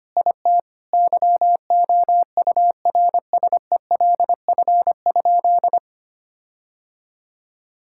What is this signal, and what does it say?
Morse code, heard as a keyed tone, about 25 wpm, 700 Hz. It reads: IT YOURSELF?
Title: Sentences from Top 1600 Words 25wpm